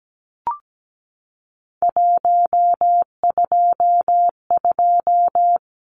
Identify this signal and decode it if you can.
Morse code, heard as a keyed tone, about 17 wpm, 700 Hz. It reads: 122
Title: Numbers - 3-Digit 17wpm